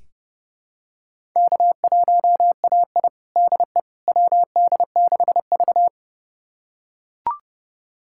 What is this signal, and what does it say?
Morse code, heard as a keyed tone, about 30 wpm, 700 Hz. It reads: K1AI DE WD6V
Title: USA Callsign DE Callsign - 4 Characters - 30wpm